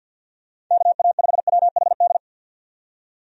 Morse code, heard as a keyed tone, about 50 wpm, 700 Hz. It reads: KA5WHD
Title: USA Call Signs - 6 Characters - 50wpm